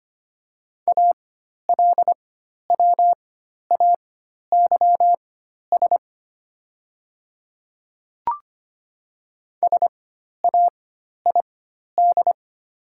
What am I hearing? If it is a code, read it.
Morse code, heard as a keyed tone, about 25 wpm, spaced as for 12 wpm, 700 Hz. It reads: ALWAYS SAID
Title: 4x Character Spacing - Top 300 Words - 25wpm